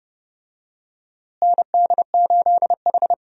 Morse code, heard as a keyed tone, about 30 wpm, 700 Hz. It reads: ND8H
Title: USA Call Signs - 4 Characters - Repeated 3x - 30wpm